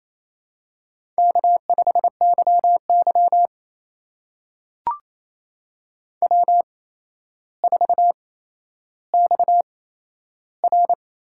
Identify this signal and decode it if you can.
Morse code, heard as a keyed tone, about 27 wpm, spaced as for 12 wpm, 700 Hz. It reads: K5YY W4XR
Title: USA Call Signs - 4 Characters - 8x Character Spacing - 28wpm